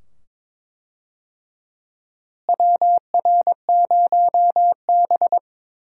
Morse code, heard as a keyed tone, about 22 wpm, 700 Hz. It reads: WR0B
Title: USA Call Signs - 4 Characters - Repeated 2x - 22wpm